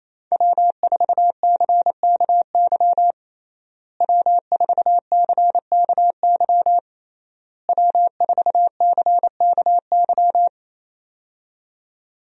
Morse code, heard as a keyed tone, about 28 wpm, 700 Hz. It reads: W4CKY W4CKY W4CKY
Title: USA Call Signs - 5 Characters - Repeated 3x - 28wpm